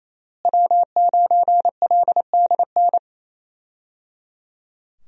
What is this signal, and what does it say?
Morse code, heard as a keyed tone, about 28 wpm, 700 Hz. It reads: W9LDN
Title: USA Call Signs - 5 Characters - Repeated 3x - 28wpm